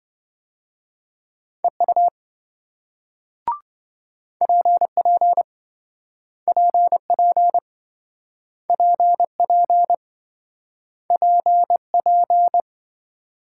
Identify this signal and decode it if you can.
Morse code, heard as a keyed tone, about 28 wpm, 700 Hz. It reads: EU PP PP PP PP